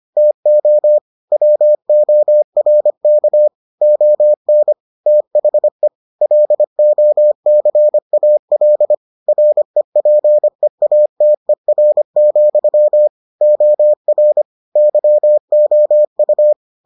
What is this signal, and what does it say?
Morse code, heard as a keyed tone, about 25 wpm, 600 Hz. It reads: TO WORK ON THE LOCAL REPEATER, OR YOU